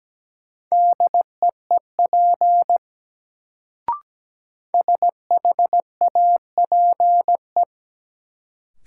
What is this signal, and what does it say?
Morse code, heard as a keyed tone, about 17 wpm, 700 Hz. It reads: DEEP SHAPE
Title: Sentences from Top 500 Words - New Word Drill - 17wpm